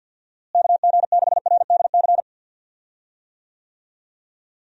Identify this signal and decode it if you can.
Morse code, heard as a keyed tone, about 50 wpm, 700 Hz. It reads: KG6RDC
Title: USA Call Signs - 6 Characters - Repeated 2x - 50wpm